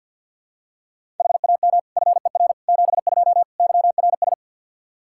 Morse code, heard as a keyed tone, about 48 wpm, 700 Hz. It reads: HAM FER 72 YRS
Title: K1USN SST QSO Practice 50wpm